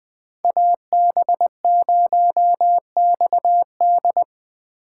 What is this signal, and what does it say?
Morse code, heard as a keyed tone, about 20 wpm, 700 Hz. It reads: AB0XD
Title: USA Call Signs - 5 Characters - 6x Character Spacing - 20wpm